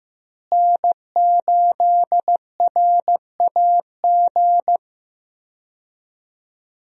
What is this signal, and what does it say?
Morse code, heard as a keyed tone, about 16 wpm, 700 Hz. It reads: N8RAG
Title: USA Call Signs - 5 Characters - Repeated 2x - 15wpm